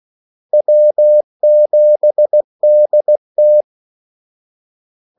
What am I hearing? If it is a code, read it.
Morse code, heard as a keyed tone, about 16 wpm, 600 Hz. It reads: W7DT
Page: Kaldesignaler 16 wpm